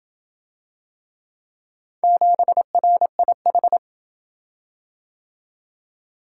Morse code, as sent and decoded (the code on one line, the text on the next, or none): --... .-. .. ....
7RIH